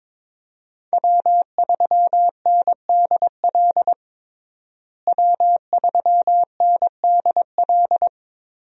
Morse code, heard as a keyed tone, about 22 wpm, 700 Hz. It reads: W3NDL W3NDL
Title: USA Call Signs - 5 Characters - Repeated 3x - 22wpm